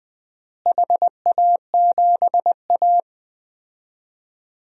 Morse code, heard as a keyed tone, about 20 wpm, 700 Hz. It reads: HA7A
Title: DXCC-100 Call Signs - 4 Characters - Repeated 2x - 20wpm